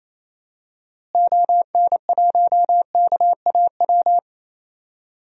Morse code, heard as a keyed tone, about 28 wpm, 700 Hz. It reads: ON1KAW
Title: DXCC-100 Call Signs - 6 Characters - Repeated 3x - 28wpm